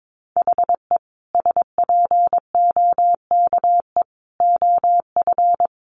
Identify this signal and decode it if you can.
Morse code, heard as a keyed tone, about 22 wpm, 700 Hz. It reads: HE SPOKE OF
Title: Sentences from Top 1200 Words 22wpm